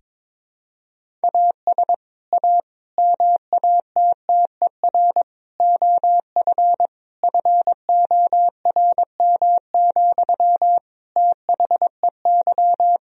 Morse code, heard as a keyed tone, about 22 wpm, 700 Hz. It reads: AS A MATTER OF FORM, THEY
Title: Sentences from Top 1000 Words - Review Part 1 - 22wpm